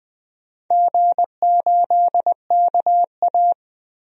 Morse code, heard as a keyed tone, about 20 wpm, 700 Hz. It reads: G8KA
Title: DXCC-100 Call Signs - 4 Characters - Repeated 3x - 20wpm